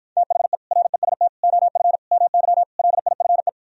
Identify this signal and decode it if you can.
Morse code, heard as a keyed tone, about 52 wpm, 700 Hz. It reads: THE REST OF MY LIFE